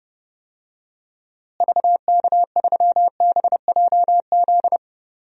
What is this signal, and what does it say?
Morse code, heard as a keyed tone, about 32 wpm, 700 Hz. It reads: VK3BJZ
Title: DXCC-100 Call Signs - 6 Characters - Repeated 3x - 30wpm